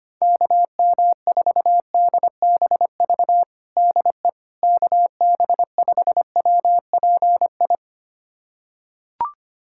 Morse code, heard as a keyed tone, about 25 wpm, 700 Hz. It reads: KM4DBV DE KB5WPI
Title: USA Callsign DE Callsign - 6 Characters - 25wpm